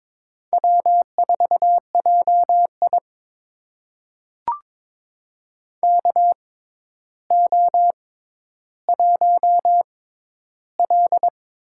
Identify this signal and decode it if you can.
Morse code, heard as a keyed tone, about 22 wpm, spaced as for 11 wpm, 700 Hz. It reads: W4JI KO1L
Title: USA Call Signs - 4 Characters - 6x Character Spacing - 22wpm